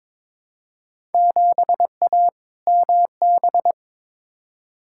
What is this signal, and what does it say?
Morse code, heard as a keyed tone, about 22 wpm, 700 Hz. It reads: 7A MB
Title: Field Day Exchanges Encore - 22wpm